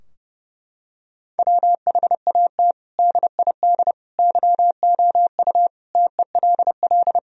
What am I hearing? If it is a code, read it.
Morse code, heard as a keyed tone, about 30 wpm, 700 Hz. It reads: WHAT DID YOU TELL
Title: Sentences from Top 1000 Words - Review Part 4 - 30wpm